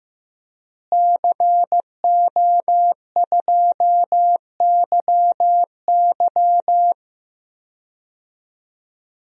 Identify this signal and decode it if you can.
Morse code, heard as a keyed tone, about 15 wpm, 700 Hz. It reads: CO2YY
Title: DXCC-100 Call Signs - 5 Characters - Repeated 3x - 15wpm